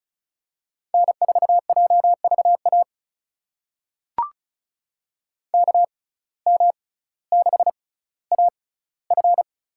Morse code, heard as a keyed tone, about 36 wpm, spaced as for 18 wpm, 700 Hz. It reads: N4JVA KM6AF
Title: USA Call Signs - 5 Characters - 6x Character Spacing - 35wpm